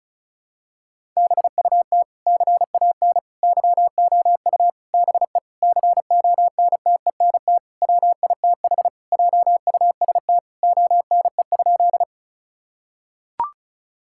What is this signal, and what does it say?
Morse code, heard as a keyed tone, about 36 wpm, 690 Hz. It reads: BUT CAN YOU BE CONTENT WITH JUST ONE?
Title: Sentences from Top 1000 Words - Review Part 6 - 35wpm